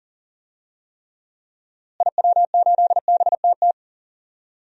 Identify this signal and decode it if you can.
Morse code, heard as a keyed tone, about 38 wpm, 700 Hz. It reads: IW8BTT
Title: DXCC-100 Call Signs - 6 Characters - Repeated 3x - 40wpm